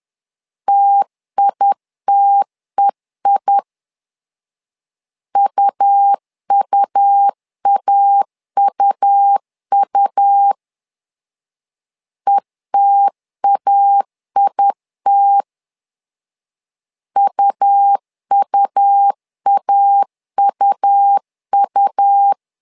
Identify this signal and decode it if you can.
Morse code, heard as a keyed tone, about 10 wpm, 800 Hz. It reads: TITEI UUAUU ETAIT UUAUU
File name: LEZIONI / 2 UHDN / 2 U 10wpm 1622